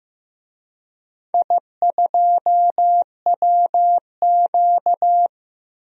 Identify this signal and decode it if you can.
Morse code, heard as a keyed tone, about 15 wpm, 700 Hz. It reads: I2WQ
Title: DXCC-100 Call Signs - 4 Characters - Repeated 3x - 15wpm